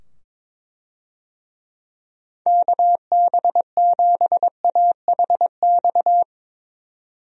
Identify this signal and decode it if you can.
Morse code, heard as a keyed tone, about 22 wpm, 700 Hz. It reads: KB7AHX
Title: USA Call Signs - 6 Characters - 22wpm